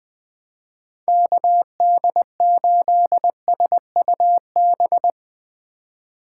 Morse code, heard as a keyed tone, about 22 wpm, 700 Hz. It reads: KD8SUB
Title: Speed Racing - USA Call Signs 20wpm